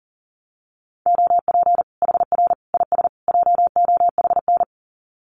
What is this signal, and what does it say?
Morse code, heard as a keyed tone, about 40 wpm, 700 Hz. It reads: OP HR IS JOHN
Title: Common QSO Phrases 40wpm